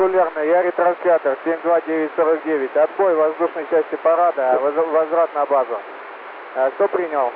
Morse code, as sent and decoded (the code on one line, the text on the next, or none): none